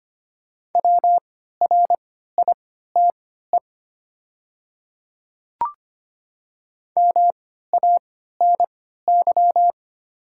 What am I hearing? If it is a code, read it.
Morse code, heard as a keyed tone, about 25 wpm, spaced as for 14 wpm, 690 Hz. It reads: WRITE MANY
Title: 3x Character Spacing - Top 100 Words - 25wpm